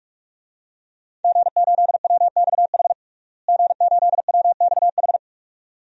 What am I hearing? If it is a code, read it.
Morse code, heard as a keyed tone, about 43 wpm, 700 Hz. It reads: G8WXH G8WXH
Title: DXCC-100 Call Signs - 5 Characters - Repeated 2x - 45wpm